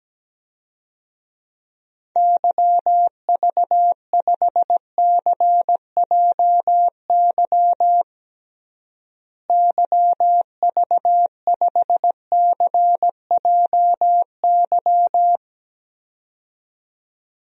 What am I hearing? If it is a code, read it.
Morse code, heard as a keyed tone, about 17 wpm, 700 Hz. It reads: YV5CJY YV5CJY
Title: DXCC-100 Call Signs - 6 Characters - Repeated 2x - 17wpm